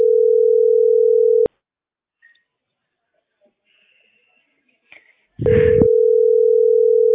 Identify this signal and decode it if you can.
Ringback tone (North America), 440 + 480 Hz.